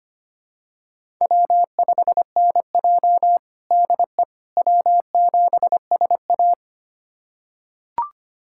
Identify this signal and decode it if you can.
Morse code, heard as a keyed tone, about 25 wpm, 700 Hz. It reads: W5NJ DE W7SA
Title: USA Callsign DE Callsign - 4 Characters - 25wpm